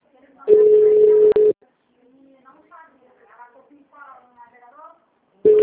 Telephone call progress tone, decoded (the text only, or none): ringback tone